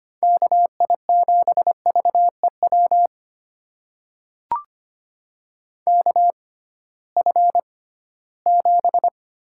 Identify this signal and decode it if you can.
Morse code, heard as a keyed tone, about 25 wpm, spaced as for 16 wpm, 700 Hz. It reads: KI7VEW KF7
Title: USA Call Signs - 6 Characters - 6x Character Spacing - 25wpm